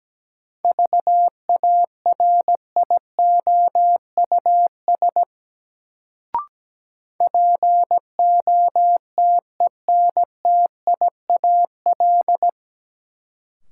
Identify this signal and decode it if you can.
Morse code, heard as a keyed tone, about 17 wpm, 700 Hz. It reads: VARIOUS POTENTIAL
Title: Sentences from Top 1000 Words - Review Part 4 - New Word Drill - 17wpm